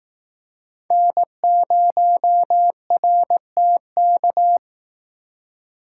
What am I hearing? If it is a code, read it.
Morse code, heard as a keyed tone, about 19 wpm, 700 Hz. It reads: N0RTK
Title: Speed Racing - USA 1x3 Call Signs 15wpm